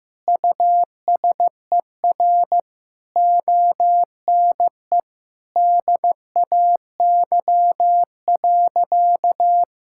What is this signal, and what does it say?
Morse code, heard as a keyed tone, about 16 wpm, 700 Hz. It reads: USER ONE DAY.
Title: Sentences from Top 1000 Words - Review Part 5 - 15wpm